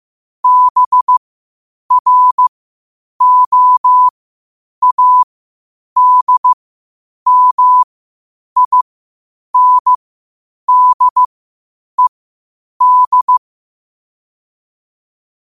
Morse code, heard as a keyed tone, about 15 wpm, spaced as for 8 wpm, 1000 Hz. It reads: BROADMINDED
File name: qotdfiles / 8wpm